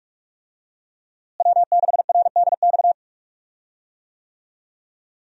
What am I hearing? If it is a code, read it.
Morse code, heard as a keyed tone, about 46 wpm, 700 Hz. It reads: W6RDX
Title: USA Call Signs - 5 Characters - Repeated 2x - 45wpm